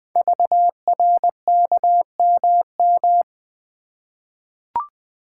Morse code, heard as a keyed tone, about 20 wpm, 700 Hz. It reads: VRKMM
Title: Code Groups - 5 Letters - 20wpm